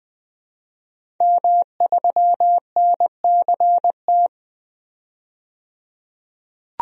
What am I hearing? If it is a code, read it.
Morse code, heard as a keyed tone, about 20 wpm, 700 Hz. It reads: M3NCT
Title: DXCC-100 Call Signs - 5 Characters - 20wpm